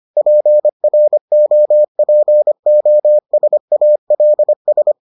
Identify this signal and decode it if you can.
Morse code, heard as a keyed tone, about 25 wpm, 600 Hz. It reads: PROPOSALS